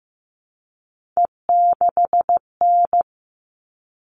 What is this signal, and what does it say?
Morse code, heard as a keyed tone, about 15 wpm, 700 Hz. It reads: E6N
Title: All Cut Numbers - 3-Digit 15wpm